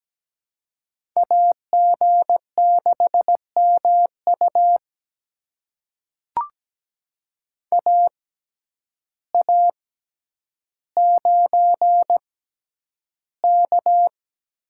Morse code, heard as a keyed tone, about 17 wpm, spaced as for 9 wpm, 700 Hz. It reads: AG6MU AA9K
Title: USA Call Signs - 5 Characters - 6x Character Spacing - 17wpm